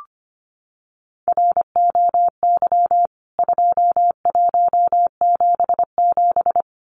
Morse code, heard as a keyed tone, about 25 wpm, 700 Hz. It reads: ROY 2177
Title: CWT Exchange 25wpm